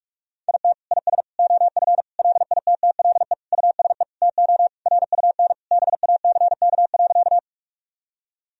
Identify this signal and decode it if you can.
Morse code, heard as a keyed tone, about 46 wpm, 700 Hz. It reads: IT IS OF LITTLE USE TO RUN BACK.
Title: Sentences from Top 1000 Words - Review Part 1 - 45wpm